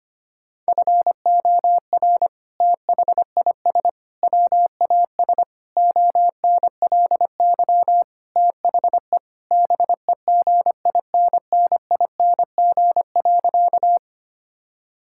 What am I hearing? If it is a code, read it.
Morse code, heard as a keyed tone, about 25 wpm, 700 Hz. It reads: FOR THIS WAS ONLY THE BEGINNING.
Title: Sentences from Top 1000 Words - Review Part 5 - 25wpm